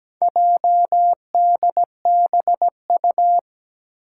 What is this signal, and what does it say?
Morse code, heard as a keyed tone, about 17 wpm, 700 Hz. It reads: JDBU
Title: Four Challenging Letter-Number Combinations 17wpm